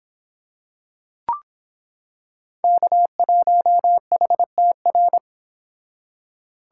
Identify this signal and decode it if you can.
Morse code, heard as a keyed tone, about 26 wpm, 700 Hz. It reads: K1HTR